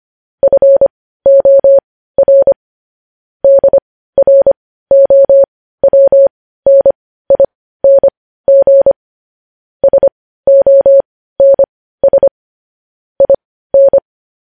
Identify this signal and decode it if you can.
Morse code, heard as a keyed tone, about 25 wpm, spaced as for 15 wpm, 550 Hz. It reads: FOR DROWNING SONS IN
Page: news-headlines-in-morse-code-at-15-wpm